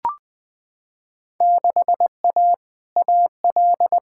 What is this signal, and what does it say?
Morse code, heard as a keyed tone, about 20 wpm, 700 Hz. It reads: 6A AL
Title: Field Day Exchanges Encore - 20wpm